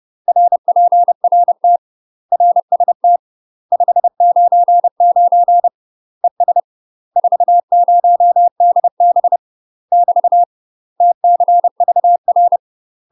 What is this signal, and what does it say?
Morse code, heard as a keyed tone, about 30 wpm, 700 Hz. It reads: RPRT RST 599 ES 40DB <BT> TCVR
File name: UK MASTER QSOs TRAINING / 30WPM / UK MASTER QSOs 0022